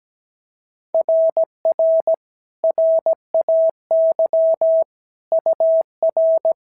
Morse code, heard as a keyed tone, about 16 wpm, 650 Hz. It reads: RR RAY UR
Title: SOTA QSO Practice 17wpm